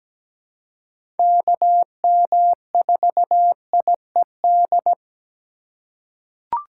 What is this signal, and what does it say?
Morse code, heard as a keyed tone, about 17 wpm, 700 Hz. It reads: KM4IED
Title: USA Call Signs - 6 Characters - 3x Character Spacing - 17wpm